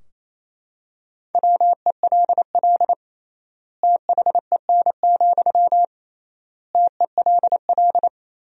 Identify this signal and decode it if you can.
Morse code, heard as a keyed tone, about 28 wpm, 700 Hz. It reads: WELL THEN, TELL
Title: 3x Word-Spacing - Sentences from Top 200 Words - 28wpm